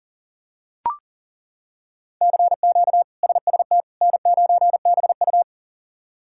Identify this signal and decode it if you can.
Morse code, heard as a keyed tone, about 40 wpm, 700 Hz. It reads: CQ SST N9BU
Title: K1USN SST QSO Practice - 40wpm